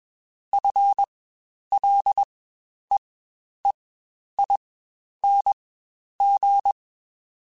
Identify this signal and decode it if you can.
Morse code, heard as a keyed tone, about 21 wpm, spaced as for 10 wpm, 800 Hz. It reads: FLEEING
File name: Spaced / Chuckles Spaced-mp3s / Temptation sp @20